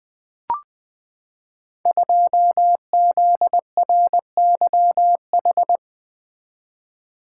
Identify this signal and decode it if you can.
Morse code, heard as a keyed tone, about 20 wpm, 700 Hz. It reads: 2ZRYH
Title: Code Groups - 5 Letter:Numbers - 20wpm